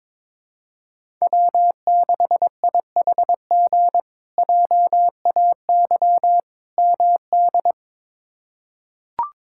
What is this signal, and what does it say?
Morse code, heard as a keyed tone, about 22 wpm, 700 Hz. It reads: W6IHG JAY MD
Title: K1USN SST QSO Practice - 22wpm